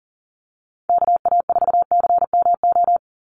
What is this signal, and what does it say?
Morse code, heard as a keyed tone, about 42 wpm, 700 Hz. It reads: KA4CMO
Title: USA Call Signs 40wpm